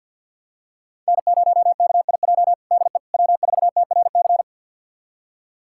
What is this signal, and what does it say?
Morse code, heard as a keyed tone, about 46 wpm, 700 Hz. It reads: N0KIJ DE W4TRC
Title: USA Callsign DE Callsign - 5 Characters - 50wpm